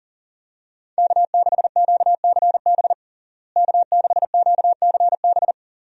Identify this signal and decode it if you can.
Morse code, heard as a keyed tone, about 37 wpm, 700 Hz. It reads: K6QCB K6QCB
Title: USA Call Signs - 5 Characters - Repeated 2x - 40wpm